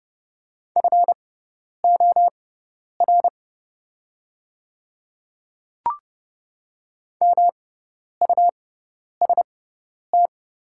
Morse code, heard as a keyed tone, about 30 wpm, spaced as for 10 wpm, 700 Hz. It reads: FOR MUST
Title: 6x Character Spacing - Top 300 Words - 30wpm